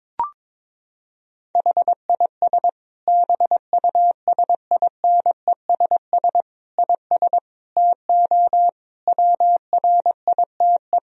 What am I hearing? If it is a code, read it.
Morse code, heard as a keyed tone, about 22 wpm, 700 Hz. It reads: HIS BUSINESS IS TO WRITE
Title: Sentences from Top 1000 Words - Review Part 3 - 22wpm